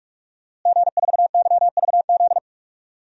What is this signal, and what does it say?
Morse code, heard as a keyed tone, about 42 wpm, 700 Hz. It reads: G4YVZ
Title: DXCC-100 Call Signs - 5 Characters - Repeated 2x - 45wpm